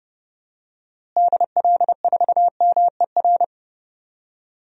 Morse code, heard as a keyed tone, about 32 wpm, 700 Hz. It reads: DL4MER
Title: DXCC-100 Call Signs - 6 Characters - Repeated 2x - 30wpm